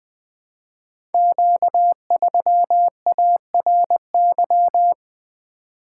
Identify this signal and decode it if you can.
Morse code, heard as a keyed tone, about 20 wpm, 700 Hz. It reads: Q3ARY
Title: Code Groups - 5 Letter:Numbers - 20wpm